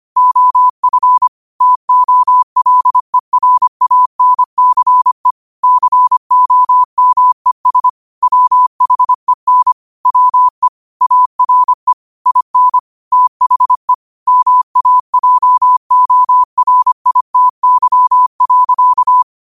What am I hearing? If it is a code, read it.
Morse code, heard as a keyed tone, about 25 wpm, 1000 Hz. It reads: OF TOLERANCE COMES WHEN WE ARE IN THE MAJORITY.
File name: qotdfiles / 25wpm